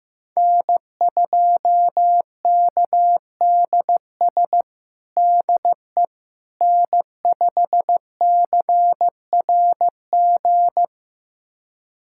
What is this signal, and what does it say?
Morse code, heard as a keyed tone, about 16 wpm, 700 Hz. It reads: N2KDS DE N5CRG
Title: USA Callsign DE Callsign - 5 Characters - 15wpm